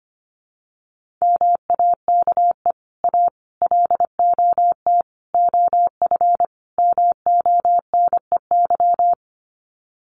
Morse code, heard as a keyed tone, about 25 wpm, 700 Hz. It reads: MAKE A LOT OF MONEY
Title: Sets of 5 Words Encore2 25wpm